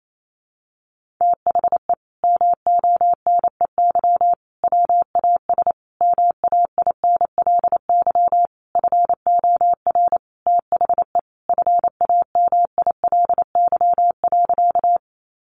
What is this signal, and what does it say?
Morse code, heard as a keyed tone, about 28 wpm, 700 Hz. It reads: THE MONEY WAS MAINLY FOR THE FAMILY.
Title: Sentences from Top 1400 Words 28wpm